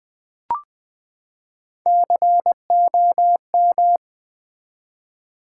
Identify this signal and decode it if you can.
Morse code, heard as a keyed tone, about 20 wpm, 700 Hz. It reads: COM
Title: Build A Word - 3 Letters At A Time - 8-Letter Words - 20wpm